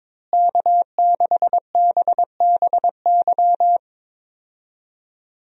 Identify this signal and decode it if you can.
Morse code, heard as a keyed tone, about 21 wpm, 700 Hz. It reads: K6BBY
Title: USA Call Signs - 5 Characters - Repeated 2x - 22wpm